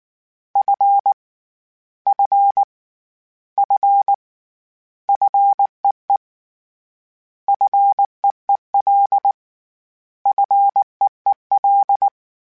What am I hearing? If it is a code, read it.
Morse code, heard as a keyed tone, about 19 wpm, 800 Hz. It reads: F F F FEE FEEL FEEL
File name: Short Runs / Short Runs-mp3s / SinglePrefixes @18